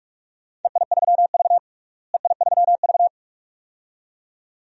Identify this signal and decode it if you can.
Morse code, heard as a keyed tone, about 46 wpm, 700 Hz. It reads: EI3V EI3V
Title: DXCC-100 Call Signs - 4 Characters - Repeated 2x - 45wpm